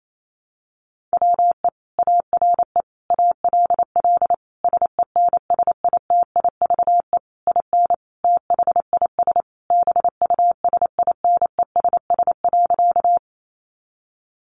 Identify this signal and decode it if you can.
Morse code, heard as a keyed tone, about 28 wpm, 700 Hz. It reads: WE ARE ALL SENSITIVE IN THIS BUSINESS.
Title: Sentences from Top 1400 Words 28wpm